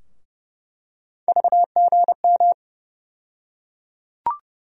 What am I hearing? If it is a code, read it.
Morse code, heard as a keyed tone, about 30 wpm, 700 Hz. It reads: VGM
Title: USA Call Sign Postfixes - 3 Letters 30wpm